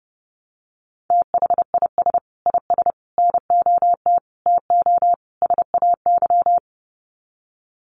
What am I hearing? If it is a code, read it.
Morse code, heard as a keyed tone, about 30 wpm, 700 Hz. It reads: THIS IS NOT TO SAY
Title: Sets of 5 Words Encore 30wpm